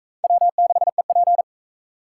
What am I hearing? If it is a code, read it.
Morse code, heard as a keyed tone, about 36 wpm, 700 Hz. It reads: W6EP